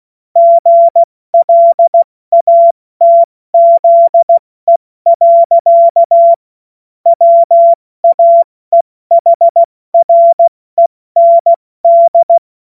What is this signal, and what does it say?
Morse code, heard as a keyed tone, about 16 wpm, 680 Hz. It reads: GLATZE. WAEHREND
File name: Morsekurs / Witze / Witz 15. Februar 2022 70 F80